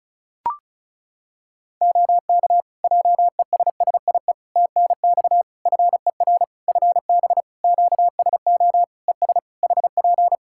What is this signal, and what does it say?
Morse code, heard as a keyed tone, about 34 wpm, 700 Hz. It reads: OK JESSIE TNX FER FB QSO ES HP
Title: Practice QSO - Exchange 4 - Ending - 35wpm